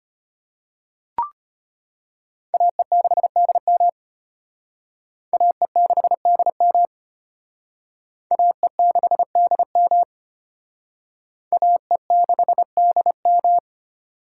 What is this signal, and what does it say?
Morse code, heard as a keyed tone, about 32 wpm, 700 Hz. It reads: AE6DM AE6DM AE6DM AE6DM